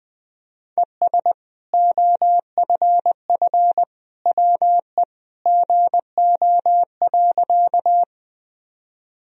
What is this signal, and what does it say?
Morse code, heard as a keyed tone, about 21 wpm, 700 Hz. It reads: ES OFF WE GO.
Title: Sentences from Top 200 Words and Abbreviations 20wpm